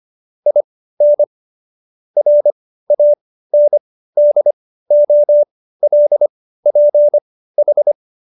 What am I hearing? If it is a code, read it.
Morse code, heard as a keyed tone, about 25 wpm, spaced as for 15 wpm, 590 Hz. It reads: IN RANDOLPH